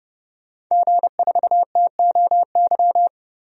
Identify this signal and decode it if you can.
Morse code, heard as a keyed tone, about 31 wpm, 700 Hz. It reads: G4TOY
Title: DXCC-100 Call Signs - 5 Characters - 30wpm